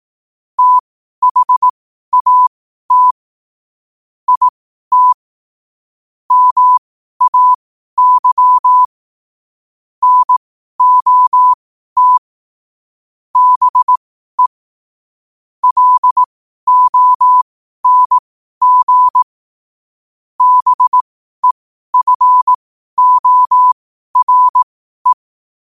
Morse code, heard as a keyed tone, about 18 wpm, spaced as for 12 wpm, 1000 Hz. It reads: THAT IT MAY NOT BE LONG BEFORE
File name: qotdfiles / 12wpm